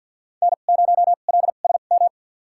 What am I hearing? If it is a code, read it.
Morse code, heard as a keyed tone, about 52 wpm, spaced as for 37 wpm, 690 Hz. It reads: N0LSM
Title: USA Call Signs - 5 Characters - 2x Character Spacing - 50wpm